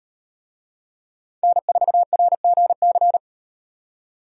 Morse code, heard as a keyed tone, about 38 wpm, 700 Hz. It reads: N4RGC